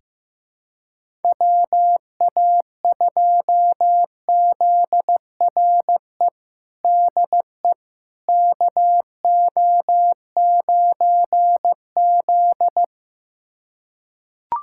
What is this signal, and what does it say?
Morse code, heard as a keyed tone, about 15 wpm, 700 Hz. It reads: WA2ZRE DE KO9Z
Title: USA Callsign DE Callsign 15wpm